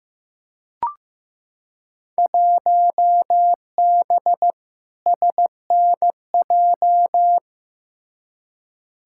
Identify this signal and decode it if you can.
Morse code, heard as a keyed tone, about 15 wpm, 700 Hz. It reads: 1B SNJ
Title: Field Day Exchanges Encore - 15wpm